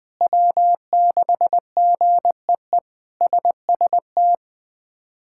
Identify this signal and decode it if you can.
Morse code, heard as a keyed tone, about 20 wpm, 700 Hz. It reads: W6GEE SST
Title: K1USN SST QSO Practice - 20wpm